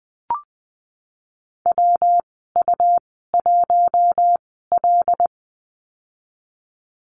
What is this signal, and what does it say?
Morse code, heard as a keyed tone, about 20 wpm, spaced as for 15 wpm, 700 Hz. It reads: WU1L
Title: 161 Lesson 26g - Letter g - Call Signs 20wpm